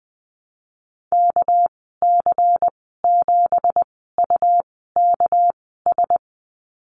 Morse code, heard as a keyed tone, about 20 wpm, spaced as for 14 wpm, 700 Hz. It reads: KC7UKS
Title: 200 Lesson 32g - Letter k - Call Signs 20wpm